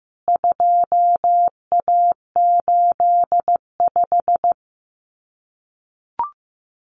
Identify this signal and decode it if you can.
Morse code, heard as a keyed tone, about 15 wpm, 700 Hz. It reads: 2A85
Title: CWT Exchange 15wpm